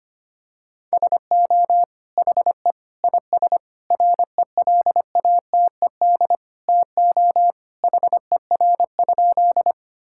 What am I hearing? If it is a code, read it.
Morse code, heard as a keyed tone, about 25 wpm, 700 Hz. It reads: SO HE IS RELATED TO HER?
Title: Sentences from Top 1000 Words - Review Part 5 - 25wpm